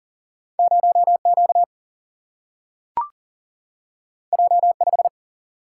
Morse code, heard as a keyed tone, about 40 wpm, 700 Hz. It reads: 0Q J5
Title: Single-Letter Single-Number 40wpm